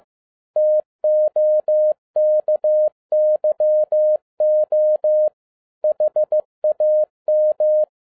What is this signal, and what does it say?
Morse code, heard as a keyed tone, about 15 wpm, 610 Hz. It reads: TOKYO HAM
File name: MY CW LIFE 1 15wpm